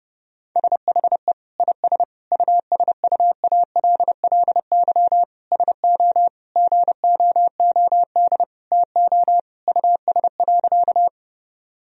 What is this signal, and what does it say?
Morse code, heard as a keyed tone, about 30 wpm, 700 Hz. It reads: SHE IS USUALLY SO GOOD TO US.
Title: Sentences from Top 1000 Words - Review Part 3 - 30wpm